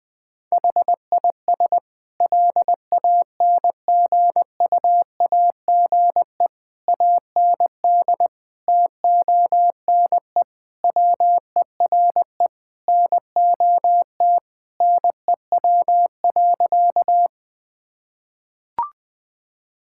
Morse code, heard as a keyed tone, about 20 wpm, 700 Hz. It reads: HIS LANGUAGE AND TONE WERE NOT NEW.
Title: Sentences from Top 1600 Words 20wpm